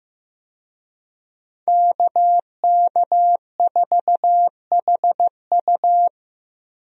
Morse code, heard as a keyed tone, about 15 wpm, 700 Hz. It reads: KK4HU
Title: USA Call Signs - 5 Characters - Repeated 3x - 15wpm